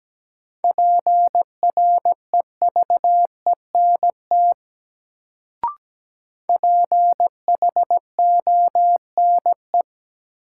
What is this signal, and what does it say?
Morse code, heard as a keyed tone, about 17 wpm, 700 Hz. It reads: PREVENT PHONE
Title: Sentences from Top 1100 Words - New Word Drill - 17wpm